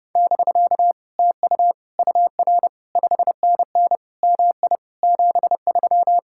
Morse code, heard as a keyed tone, about 28 wpm, 700 Hz. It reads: <BK> TU UR 5NN MI 73